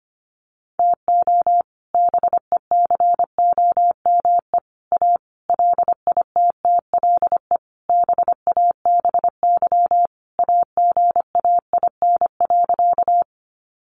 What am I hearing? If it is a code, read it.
Morse code, heard as a keyed tone, about 25 wpm, 700 Hz. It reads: TO BECOME A LITTLE BABY AGAIN.
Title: Sentences from Top 1300 Words 25wpm